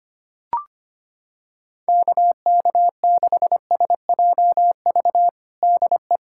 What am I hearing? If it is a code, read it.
Morse code, heard as a keyed tone, about 25 wpm, 700 Hz. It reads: KK6SJV DE
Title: USA Callsign DE Callsign - 6 Characters - 25wpm